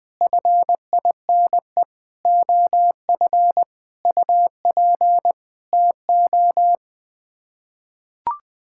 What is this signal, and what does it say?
Morse code, heard as a keyed tone, about 21 wpm, 700 Hz. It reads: FINE OF UP TO
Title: Sets of 9 Words - 20wpm